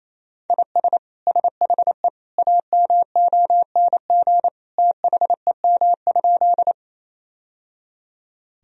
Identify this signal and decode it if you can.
Morse code, heard as a keyed tone, about 28 wpm, 700 Hz. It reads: IS SHE AMONG THEM?
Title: Sentences from Top 1000 Words - Review Part 3 - 28wpm